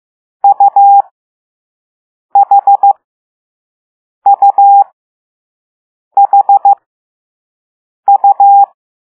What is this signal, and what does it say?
Morse code, heard as a keyed tone, about 15 wpm, spaced as for 5 wpm, 800 Hz. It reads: UHUHU